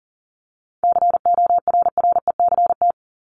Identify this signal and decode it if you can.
Morse code, heard as a keyed tone, about 40 wpm, 700 Hz. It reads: CORRECT
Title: Top 500 Words 40wpm